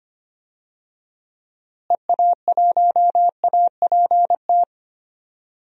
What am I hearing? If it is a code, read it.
Morse code, heard as a keyed tone, about 25 wpm, 700 Hz. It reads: EA1APT
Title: DXCC-100 Call Signs - 6 Characters - Repeated 3x - 25wpm